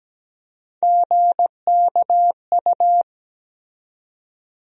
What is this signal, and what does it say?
Morse code, heard as a keyed tone, about 17 wpm, 700 Hz. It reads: GKU
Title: USA Call Sign Postfixes - 3 Letters 17wpm